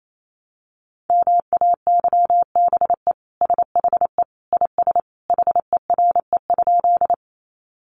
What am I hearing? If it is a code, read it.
Morse code, heard as a keyed tone, about 28 wpm, 700 Hz. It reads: MAYBE SHE IS HERE?
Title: Sentences from Top 1000 Words 28wpm